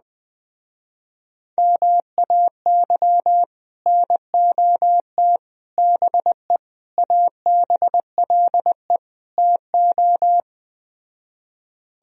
Morse code, heard as a keyed tone, about 20 wpm, 700 Hz. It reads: MAY NOT BE ABLE TO
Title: Speed-Racing - Sets of 5 Words 17wpm